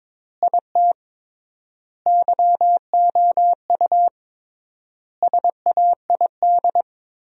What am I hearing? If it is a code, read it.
Morse code, heard as a keyed tone, about 22 wpm, 700 Hz. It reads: IT YOU SAID
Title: 3x Word-Spacing - Sentences from Top 200 Words - 22wpm